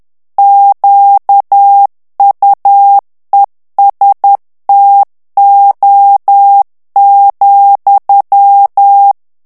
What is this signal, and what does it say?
Morse code, heard as a keyed tone, about 10 wpm, 800 Hz. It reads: QUESTO,
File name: ITALIEN texte clair 2 vitesse 10 mots minute